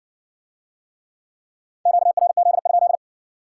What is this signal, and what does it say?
Morse code, heard as a keyed tone, about 60 wpm, 700 Hz. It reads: QRZ?